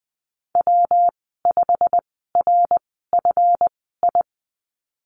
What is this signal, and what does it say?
Morse code, heard as a keyed tone, about 20 wpm, spaced as for 15 wpm, 700 Hz. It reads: W5RFI
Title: 140 Lesson 23g - Letter f - Call Signs 20wpm